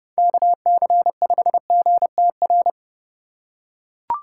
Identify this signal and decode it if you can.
Morse code, heard as a keyed tone, about 28 wpm, 700 Hz. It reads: KC5GTR